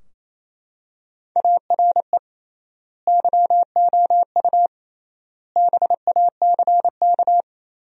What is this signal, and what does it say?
Morse code, heard as a keyed tone, about 28 wpm, 700 Hz. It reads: ARE YOU BACK
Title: 3x Word-Spacing - Sentences from Top 200 Words - 28wpm